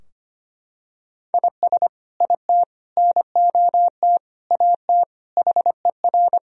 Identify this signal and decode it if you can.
Morse code, heard as a keyed tone, about 25 wpm, 700 Hz. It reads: IS IT NOT AT HER
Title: Sentences from Top 1000 Words - Review Part 4 - 25wpm